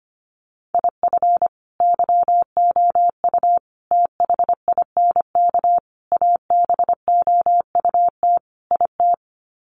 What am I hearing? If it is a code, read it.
Morse code, heard as a keyed tone, about 25 wpm, 700 Hz. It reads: IF YOU THINK ABOUT IT
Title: Sets of 5 Words Encore2 25wpm